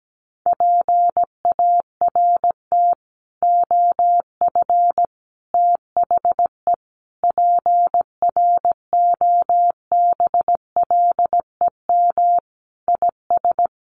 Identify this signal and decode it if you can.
Morse code, heard as a keyed tone, about 17 wpm, 700 Hz. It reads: PART OF THE PROBLEM IS
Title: Sets of 5 Words Encore2 17wpm